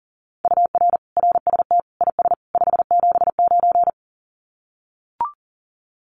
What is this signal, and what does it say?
Morse code, heard as a keyed tone, about 40 wpm, 700 Hz. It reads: UR RST IS 579
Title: Common QSO Phrases 40wpm